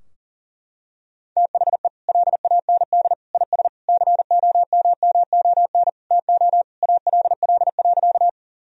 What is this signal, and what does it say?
Morse code, heard as a keyed tone, about 37 wpm, 700 Hz. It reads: THE LAND IS COMMON TO ALL.
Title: Sentences from Top 1000 Words - Review Part 3 - 40wpm